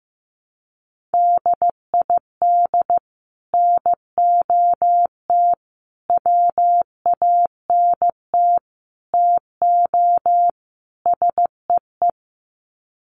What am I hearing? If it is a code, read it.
Morse code, heard as a keyed tone, about 14 wpm, 700 Hz. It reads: DID NOT WANT TO SEE
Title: Sets of 5 Words Encore2 15wpm